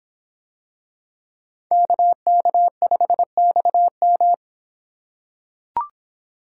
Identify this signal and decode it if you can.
Morse code, heard as a keyed tone, about 26 wpm, 700 Hz. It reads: KK5XM